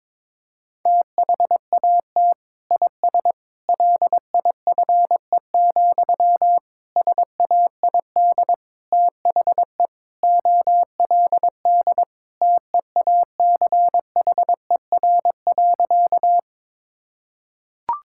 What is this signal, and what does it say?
Morse code, heard as a keyed tone, about 22 wpm, 700 Hz. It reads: THAT IS LIFE, SAID THE OLD TEACHER.
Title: Sentences from Top 1000 Words - Review Part 6 - 22wpm